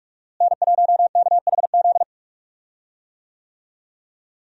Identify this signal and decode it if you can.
Morse code, heard as a keyed tone, about 46 wpm, 700 Hz. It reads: N1KHZ
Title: USA Call Signs - 5 Characters - Repeated 2x - 45wpm